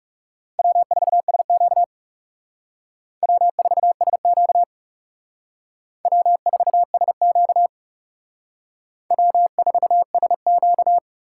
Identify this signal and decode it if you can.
Morse code, heard as a keyed tone, about 39 wpm, 700 Hz. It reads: W4SQ W4SQ W4SQ W4SQ